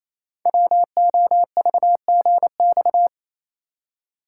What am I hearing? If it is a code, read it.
Morse code, heard as a keyed tone, about 28 wpm, 700 Hz. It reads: WOVGX
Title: Code Groups - 5 Letters - 28wpm